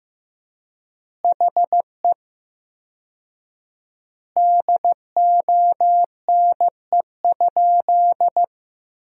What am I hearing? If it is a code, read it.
Morse code, heard as a keyed tone, about 16 wpm, 700 Hz. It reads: HE DONE?
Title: 4x Word-Spacing - Sentences from Top 500 Words - 15wpm